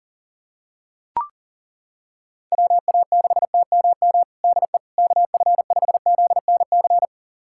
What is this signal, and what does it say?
Morse code, heard as a keyed tone, about 40 wpm, 700 Hz. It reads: WA6TMM DE KF5ZNC
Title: USA Callsign DE Callsign - 6 Characters - 40wpm